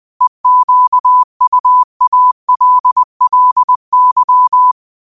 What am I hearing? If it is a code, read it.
Morse code, heard as a keyed tone, about 20 wpm, 1000 Hz. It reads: EQUALLY